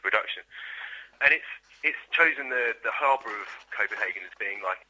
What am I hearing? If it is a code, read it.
Emergency Alert System (SAME) tones, heard as harsh buzzing bursts, with speech. It is heard from 0.5 to 4.3 s.